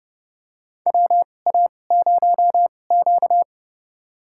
Morse code, heard as a keyed tone, about 29 wpm, spaced as for 21 wpm, 700 Hz. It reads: WA0Q
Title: USA Call Signs - 4 Characters - 2x Character Spacing - 30wpm